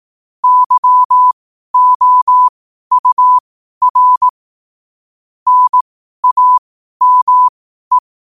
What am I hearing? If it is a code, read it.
Morse code, heard as a keyed tone, about 18 wpm, spaced as for 12 wpm, 1000 Hz. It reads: YOUR NAME
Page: New quotes every day in morse code at 12 Words per minute.